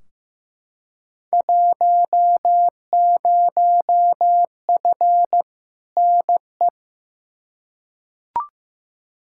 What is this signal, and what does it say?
Morse code, heard as a keyed tone, about 15 wpm, 700 Hz. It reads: 10F NE
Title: Field Day Exchanges Encore - 15wpm